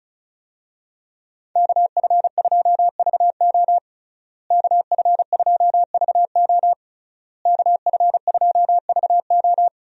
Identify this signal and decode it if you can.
Morse code, heard as a keyed tone, about 35 wpm, 700 Hz. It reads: KF2VO KF2VO KF2VO
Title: USA Call Signs - 5 Characters - Repeated 3x - 35wpm